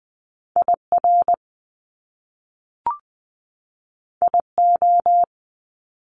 Morse code, heard as a keyed tone, about 20 wpm, 700 Hz. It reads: IR IO
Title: Top Two Letter Combinations in Words 20wpm